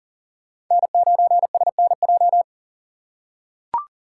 Morse code, heard as a keyed tone, about 37 wpm, 690 Hz. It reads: N9SNJ
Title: USA Call Signs - 5 Characters - 6x Character Spacing - 40wpm